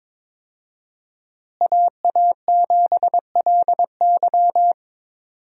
Morse code, heard as a keyed tone, about 22 wpm, 690 Hz. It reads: AA7LY
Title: USA Call Signs - 5 Characters - Repeated 3x - 22wpm